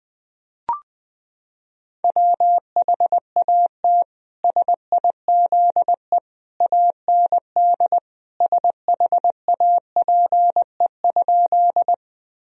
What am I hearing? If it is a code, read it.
Morse code, heard as a keyed tone, about 20 wpm, 700 Hz. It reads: WHAT SIZE AND SHAPE?
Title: Sentences from Top 1000 Words - Review Part 3 - 20wpm